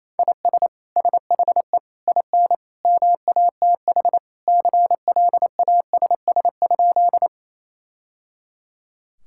Morse code, heard as a keyed tone, about 28 wpm, 700 Hz. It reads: IS SHE IN MATH CLASS?
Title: Sentences from Top 1000 Words - Review Part 6 - 28wpm